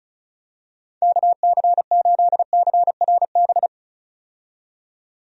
Morse code, heard as a keyed tone, about 36 wpm, 700 Hz. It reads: KC8CRB
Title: USA Call Signs - 6 Characters - 35wpm